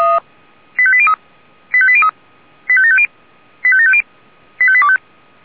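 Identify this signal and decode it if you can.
Five-tone selective call (ZVEI), heard as quick runs of five tones, tones of about 70 ms. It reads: 85803 85803 85750 85750 85825
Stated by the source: ZVEI five-tone selective call received with an SDR.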